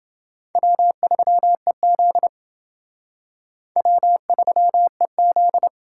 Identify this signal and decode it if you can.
Morse code, heard as a keyed tone, about 29 wpm, 700 Hz. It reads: W3EZ W3EZ